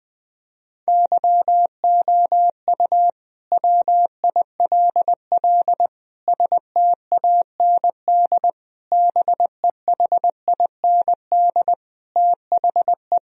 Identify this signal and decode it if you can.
Morse code, heard as a keyed tone, about 20 wpm, 700 Hz. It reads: YOU WILL STAND BEHIND THE
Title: Sentences from Top 1000 Words - Review Part 4 - 20wpm